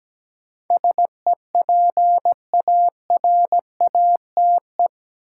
Morse code, heard as a keyed tone, about 17 wpm, 700 Hz. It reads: SEPARATE
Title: Sentences from Top 1000 Words - Review Part 7 - New Word Drill - 17wpm